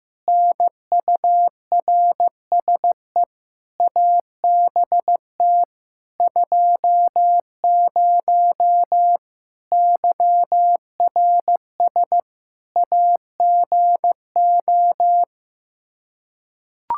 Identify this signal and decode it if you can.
Morse code, heard as a keyed tone, about 15 wpm, 700 Hz. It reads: NURSE ABT 20 YRS AGO
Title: Practice QSO - Exchange 3 - More Info - 15wpm